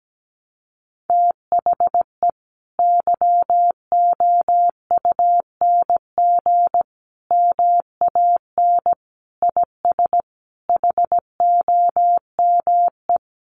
Morse code, heard as a keyed tone, about 17 wpm, 700 Hz. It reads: THE YOUNG MAN IS HOME
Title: Sentences from Top 1300 Words 17wpm